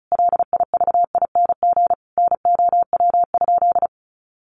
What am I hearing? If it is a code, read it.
Morse code, heard as a keyed tone, about 34 wpm, 700 Hz. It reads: LIVING NOW?
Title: Sentences from Top 1100 Words 35wpm